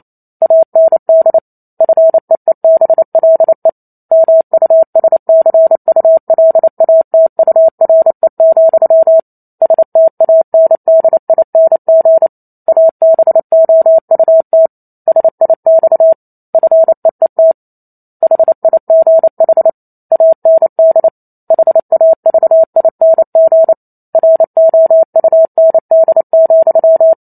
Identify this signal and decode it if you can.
Morse code, heard as a keyed tone, about 28 wpm, 650 Hz. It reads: AND FEEBLE MUSCULATURE, STANDING ABOUT SIX FEET HIGH AND HAVING ROUND,